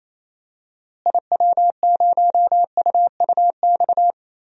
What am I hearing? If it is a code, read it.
Morse code, heard as a keyed tone, about 28 wpm, 700 Hz. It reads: IW0UUX
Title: DXCC-100 Call Signs - 6 Characters - Repeated 2x - 28wpm